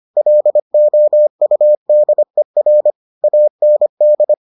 Morse code, heard as a keyed tone, about 25 wpm, 600 Hz. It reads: LOUDER AND